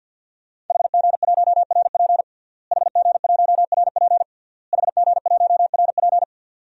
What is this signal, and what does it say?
Morse code, heard as a keyed tone, about 52 wpm, 700 Hz. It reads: HG1RP HG1RP HG1RP
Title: DXCC-100 Call Signs - 5 Characters - Repeated 3x - 50wpm